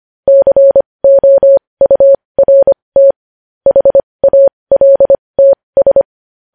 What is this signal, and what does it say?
Morse code, heard as a keyed tone, about 25 wpm, spaced as for 20 wpm, 550 Hz. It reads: COURT HALTS
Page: news-headlines-in-morse-code-at-20-wpm